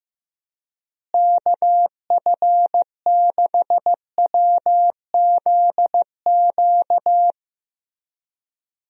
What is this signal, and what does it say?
Morse code, heard as a keyed tone, about 15 wpm, 700 Hz. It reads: KF6WZQ
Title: USA Call Signs - 6 Characters - Repeated 3x - 15wpm